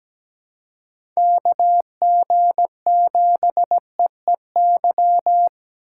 Morse code, heard as a keyed tone, about 17 wpm, 700 Hz. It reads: KG7EEY
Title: USA Call Signs - 3x Character Spacing - 17wpm